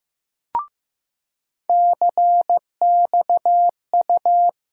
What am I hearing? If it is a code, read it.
Morse code, heard as a keyed tone, about 15 wpm, 700 Hz. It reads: CXU
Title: USA Call Sign Postfixes - 3 Letters 15wpm